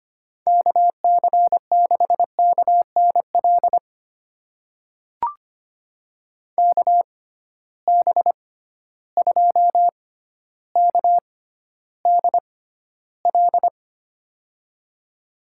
Morse code, heard as a keyed tone, about 25 wpm, spaced as for 13 wpm, 700 Hz. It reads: KC6KNL KB2KDL
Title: USA Call Signs - 6 Characters - 6x Character Spacing - 25wpm